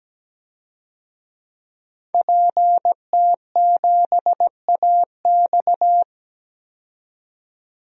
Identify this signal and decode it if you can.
Morse code, heard as a keyed tone, about 17 wpm, 700 Hz. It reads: PT7AX
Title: DXCC-100 Call Signs - 5 Characters - 17wpm